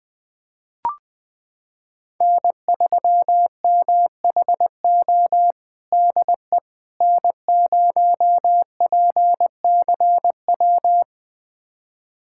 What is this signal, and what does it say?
Morse code, heard as a keyed tone, about 20 wpm, 700 Hz. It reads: N3MHO DE N0PCW
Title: USA Callsign DE Callsign - 5 Characters - 20wpm